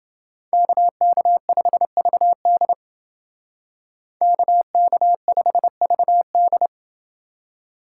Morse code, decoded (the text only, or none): KK5VD KK5VD